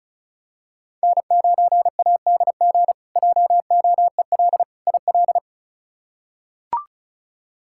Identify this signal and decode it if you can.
Morse code, heard as a keyed tone, about 36 wpm, 700 Hz. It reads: N9ADG JOEL IL
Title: K1USN SST QSO Practice - 35wpm